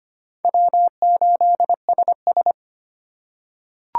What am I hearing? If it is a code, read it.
Morse code, heard as a keyed tone, about 25 wpm, 700 Hz. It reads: W8SS
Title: USA Call Signs - 3x Character Spacing - 25wpm